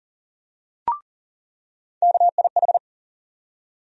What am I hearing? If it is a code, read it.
Morse code, heard as a keyed tone, about 40 wpm, 700 Hz. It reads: KIH
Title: USA Call Sign Postfixes - 3 Letters 40wpm